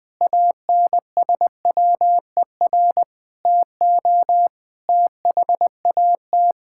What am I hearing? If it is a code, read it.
Morse code, heard as a keyed tone, about 20 wpm, 700 Hz. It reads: ANSWER TO THAT
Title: Sentences with 7-Word Phrases - Spoken Beginning - 20wpm